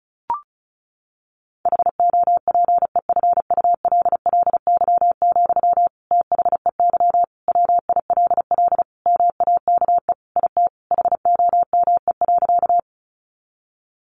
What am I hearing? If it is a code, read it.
Morse code, heard as a keyed tone, about 34 wpm, 700 Hz. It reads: HOPEFULLY, THEY WILL MAKE IT HOME.
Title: Sentences from Top 1500 Words 35wpm